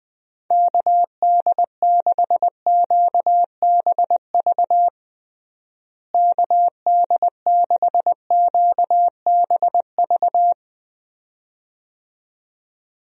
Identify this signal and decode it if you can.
Morse code, heard as a keyed tone, about 21 wpm, 700 Hz. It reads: KD6QBV KD6QBV
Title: USA Call Signs - 6 Characters - Repeated 2x - 20wpm